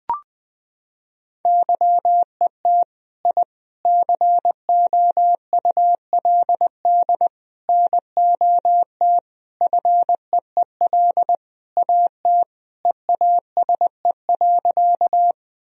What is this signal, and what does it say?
Morse code, heard as a keyed tone, about 20 wpm, 700 Hz. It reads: YET I COULD NOT FEEL AT EASE.
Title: Sentences from Top 1000 Words - Review Part 1 - 20wpm